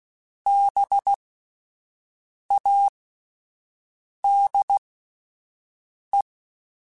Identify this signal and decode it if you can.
Morse code, heard as a keyed tone, about 16 wpm, spaced as for 5 wpm, 800 Hz. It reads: BADE